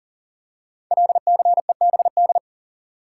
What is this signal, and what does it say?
Morse code, heard as a keyed tone, about 40 wpm, 690 Hz. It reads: LCEBD
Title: Code Groups - 5 Letters - 40wpm